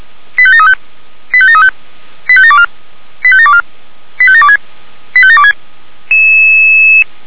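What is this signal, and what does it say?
Five-tone selective call (ZVEI), heard as quick runs of five tones, tones of about 70 ms. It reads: 85735 85735 85724 85724 85726 85726
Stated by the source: ZVEI five-tone call received on SDR.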